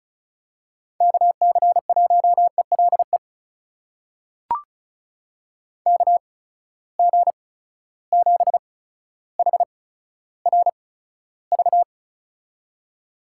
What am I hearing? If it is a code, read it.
Morse code, heard as a keyed tone, about 36 wpm, spaced as for 15 wpm, 700 Hz. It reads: KC1ELE KG7HRV
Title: USA Call Signs - 6 Characters - 8x Character Spacing - 35wpm